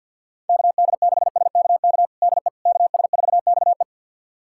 Morse code, heard as a keyed tone, about 51 wpm, 700 Hz. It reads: KD6SKK DE KS4XE
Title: USA Callsign DE Callsign 50wpm